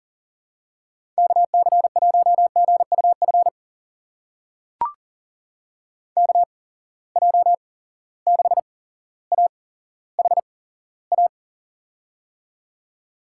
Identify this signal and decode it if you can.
Morse code, heard as a keyed tone, about 40 wpm, spaced as for 17 wpm, 700 Hz. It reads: KC1GUF KJ6AHA
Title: USA Call Signs - 6 Characters - 8x Character Spacing - 40wpm